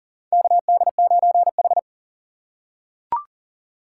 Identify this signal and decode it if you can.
Morse code, heard as a keyed tone, about 40 wpm, 690 Hz. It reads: KD9H
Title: USA Call Signs - 4x Character Spacing - 40wpm